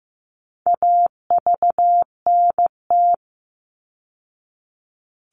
Morse code, heard as a keyed tone, about 15 wpm, 700 Hz. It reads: AVNT
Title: All Cut Numbers - 4-Digit 15wpm